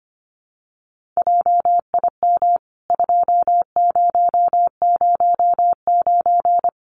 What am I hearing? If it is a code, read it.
Morse code, heard as a keyed tone, about 24 wpm, 700 Hz. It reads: JIM 2009
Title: CWT Exchange 25wpm